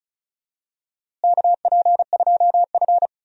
Morse code, heard as a keyed tone, about 36 wpm, 700 Hz. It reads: KP2F
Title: DXCC-100 Call Signs - 4 Characters - Repeated 2x - 35wpm